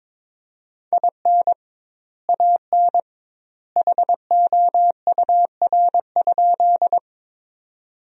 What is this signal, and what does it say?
Morse code, heard as a keyed tone, about 21 wpm, 700 Hz. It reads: IN AN HOUR?
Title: 2x Word-Spacing - Sentences from Top 400 Words - 22wpm